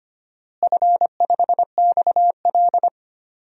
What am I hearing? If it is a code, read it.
Morse code, heard as a keyed tone, about 25 wpm, 700 Hz. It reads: F5XL
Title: DXCC-100 Call Signs - 4 Characters - Repeated 3x - 25wpm